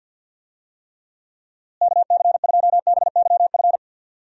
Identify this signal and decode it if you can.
Morse code, heard as a keyed tone, about 47 wpm, 700 Hz. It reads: KC2BYF
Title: USA Call Signs - 6 Characters - Repeated 3x - 50wpm